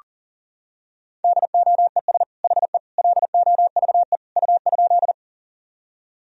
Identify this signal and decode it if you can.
Morse code, heard as a keyed tone, about 40 wpm, 700 Hz. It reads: DOES HE LOVE U?
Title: Sentences from Top 400 Words and Abbreviations 40wpm